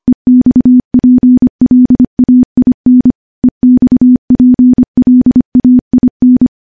Morse code, heard as a keyed tone, about 25 wpm, 250 Hz. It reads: EXPLAIN EXPLAIN
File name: morse / morse-training-25WPM